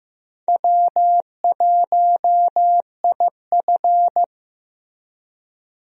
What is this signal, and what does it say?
Morse code, heard as a keyed tone, about 15 wpm, 700 Hz. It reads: W1IF
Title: USA Call Signs - 4 Characters - Repeated 3x - 15wpm